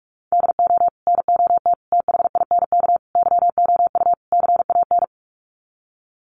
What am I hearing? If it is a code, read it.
Morse code, heard as a keyed tone, about 44 wpm, 700 Hz. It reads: DO NOT THINK YOU CAN
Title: Sets of 5 Words 45wpm